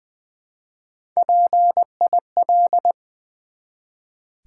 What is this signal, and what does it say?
Morse code, heard as a keyed tone, about 20 wpm, 700 Hz. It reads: PIL
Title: Build A Word - 1 Letter At A Time - 5-Letter Words - 20wpm